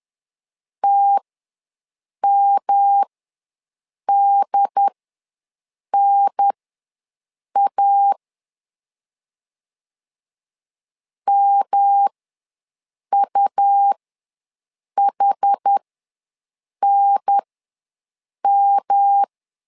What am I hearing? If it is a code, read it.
Morse code, heard as a keyed tone, about 10 wpm, spaced as for 6 wpm, 800 Hz. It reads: TMDNA MUHNM
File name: LEZIONI / 3 MWRO / 3 M ws 10wpm 2121